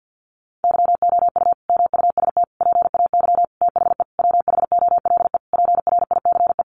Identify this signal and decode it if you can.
Morse code, heard as a keyed tone, about 50 wpm, 700 Hz. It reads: YOU MUST PAY THE WHOLE PRICE
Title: Sentences from Top 1400 Words 50wpm